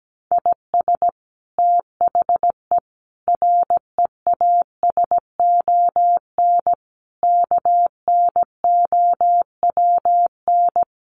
Morse code, heard as a keyed tone, about 17 wpm, 700 Hz. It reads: IS THE REASON KNOWN
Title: Sentences from Top 700 Words 17wpm